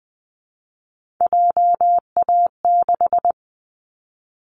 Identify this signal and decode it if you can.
Morse code, heard as a keyed tone, about 20 wpm, 700 Hz. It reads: JA6
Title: Callsign Prefixes for Top 25 DXCC - Repeated 3x - 20wpm